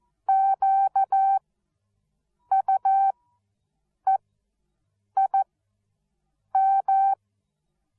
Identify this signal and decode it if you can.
Morse code, heard as a keyed tone, about 14 wpm, spaced as for 6 wpm, 780 Hz. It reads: QUEIM
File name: Onlinelehrgang / morsen / DJ4UF Morsen13